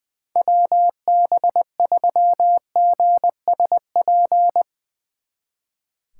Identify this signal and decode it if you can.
Morse code, heard as a keyed tone, about 20 wpm, 700 Hz. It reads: WB3GSP
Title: Speed Racing - USA Call Signs 20wpm